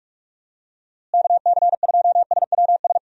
Morse code, heard as a keyed tone, about 46 wpm, 700 Hz. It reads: KC2SWS
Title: USA Call Signs - 6 Characters - Repeated 3x - 45wpm